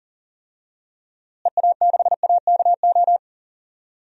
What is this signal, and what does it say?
Morse code, heard as a keyed tone, about 40 wpm, 700 Hz. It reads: EA6AKO
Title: DXCC-100 Call Signs - 6 Characters - Repeated 3x - 40wpm